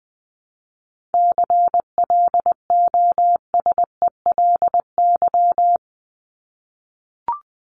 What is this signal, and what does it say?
Morse code, heard as a keyed tone, about 20 wpm, 700 Hz. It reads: CLOSELY
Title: Top 1251-1500 Words 20wpm